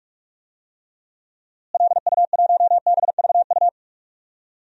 Morse code, heard as a keyed tone, about 41 wpm, 700 Hz. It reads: LU1BVU
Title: DXCC-100 Call Signs - 6 Characters - Repeated 3x - 45wpm